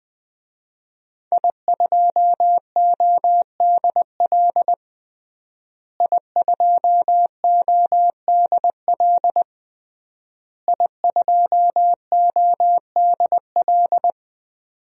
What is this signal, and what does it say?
Morse code, heard as a keyed tone, about 20 wpm, 700 Hz. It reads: I2ODL I2ODL I2ODL
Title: DXCC-100 Call Signs - 5 Characters - Repeated 3x - 20wpm